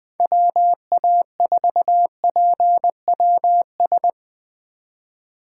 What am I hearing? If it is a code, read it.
Morse code, heard as a keyed tone, about 21 wpm, 700 Hz. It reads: WA4PWS